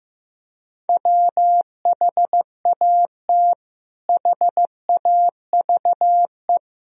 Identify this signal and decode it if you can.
Morse code, heard as a keyed tone, about 15 wpm, 700 Hz. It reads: WHAT HAVE
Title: Sentences from Top 1000 Words - Review Part 2 - 15wpm